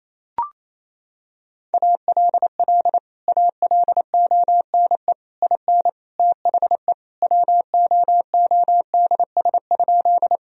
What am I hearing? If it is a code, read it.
Morse code, heard as a keyed tone, about 27 wpm, 700 Hz. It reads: ALL ALONE IN THE WOODS?
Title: Sentences from Top 1000 Words - Review Part 2 - 28wpm